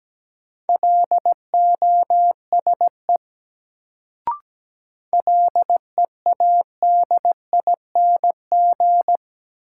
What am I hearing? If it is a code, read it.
Morse code, heard as a keyed tone, about 17 wpm, 700 Hz. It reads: LOSE LEADING
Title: Sentences from Top 1100 Words - New Word Drill Alt - 17wpm